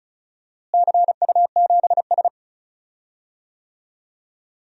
Morse code, heard as a keyed tone, about 36 wpm, 690 Hz. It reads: CU7S
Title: DXCC-100 Call Signs - 4 Characters - Repeated 2x - 35wpm